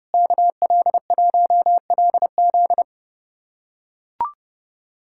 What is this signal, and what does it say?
Morse code, heard as a keyed tone, about 29 wpm, 700 Hz. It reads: KL1LZ
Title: USA Callsign DE Callsign - 5 Characters - 30wpm